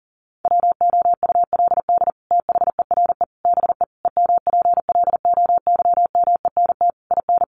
Morse code, heard as a keyed tone, about 40 wpm, 700 Hz. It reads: WOULD THERE BE EMPLOYMENT IN
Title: Sentences from Top 1500 Words 40wpm